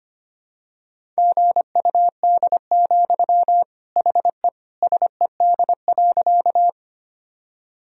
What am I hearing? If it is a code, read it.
Morse code, heard as a keyed tone, about 25 wpm, 700 Hz. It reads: GUD, HE SED.
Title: Sentences from Top 200 Words and Abbreviations 25wpm